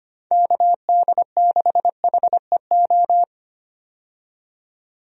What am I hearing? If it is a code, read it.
Morse code, heard as a keyed tone, about 25 wpm, 700 Hz. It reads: KD6HEO
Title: USA Call Signs - 6 Characters - 25wpm